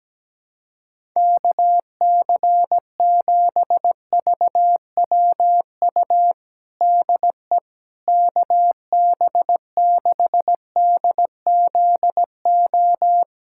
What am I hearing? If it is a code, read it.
Morse code, heard as a keyed tone, about 17 wpm, 700 Hz. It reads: KC7VWU DE KB6DZO
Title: USA Callsign DE Callsign 17wpm